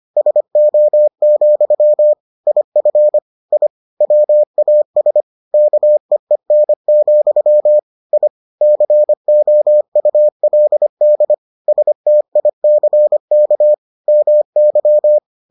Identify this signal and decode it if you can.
Morse code, heard as a keyed tone, about 25 wpm, 600 Hz. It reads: SO, IF I WAS KEEN, I COULD STICK MY